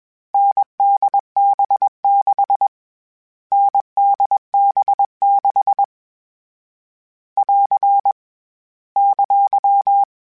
Morse code, heard as a keyed tone, about 21 wpm, 800 Hz. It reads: NDB6 NDB6 <AR> !
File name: Variable Speed / Variable Speed-mp3s / GZ GZ7 sp @20